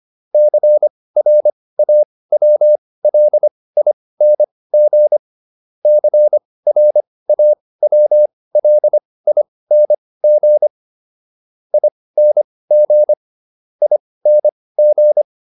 Morse code, heard as a keyed tone, about 25 wpm, spaced as for 18 wpm, 600 Hz. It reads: CRAWLING CRAWLING ING ING
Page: intermediate-practice-files